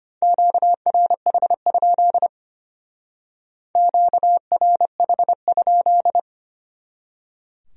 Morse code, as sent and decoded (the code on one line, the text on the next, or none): --.- .-. .... ..--.. / --.- .-. .... ..--..
QRH? QRH?